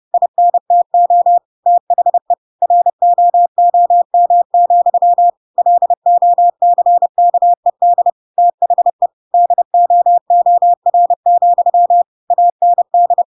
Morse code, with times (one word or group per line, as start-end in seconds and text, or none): 0.1-1.4 INTO
1.7-2.3 THE
2.6-5.3 ROOM,
5.6-8.1 LOCKED
8.4-9.1 THE
9.3-12.0 DOOR,
12.3-13.2 AND